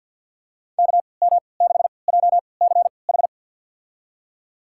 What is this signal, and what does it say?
Morse code, heard as a keyed tone, about 53 wpm, spaced as for 29 wpm, 700 Hz. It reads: KM6JCH
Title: USA Call Signs - 6 Characters - 3x Character Spacing - 50wpm